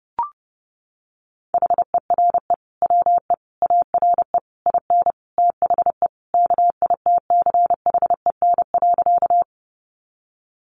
Morse code, heard as a keyed tone, about 30 wpm, 700 Hz. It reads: HERE WE ARE IN THE KITCHEN.
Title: Sentences from Top 1400 Words 30wpm